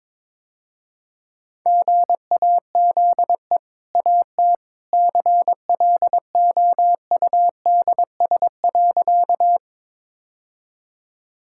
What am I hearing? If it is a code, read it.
Morse code, heard as a keyed tone, about 21 wpm, 700 Hz. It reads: GAZE AT CLOUDS.
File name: practice / 22wpm / Easy Sentences with Hints - 22wpm